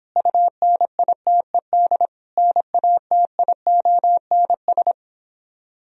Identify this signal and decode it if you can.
Morse code, heard as a keyed tone, about 26 wpm, 700 Hz. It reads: UNITED NATIONS